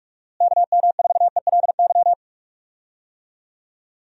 Morse code, heard as a keyed tone, about 46 wpm, 700 Hz. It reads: KM4ELY
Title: USA Call Signs - 6 Characters - Repeated 2x - 45wpm